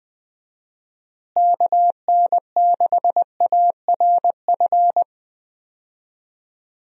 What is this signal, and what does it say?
Morse code, heard as a keyed tone, about 20 wpm, 700 Hz. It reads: KN6ARF
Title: USA Call Signs - 6 Characters - Repeated 3x - 20wpm